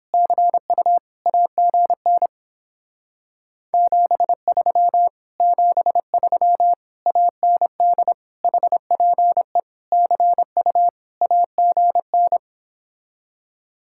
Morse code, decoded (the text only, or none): CU AGN 73 73 AND HPE CU AGN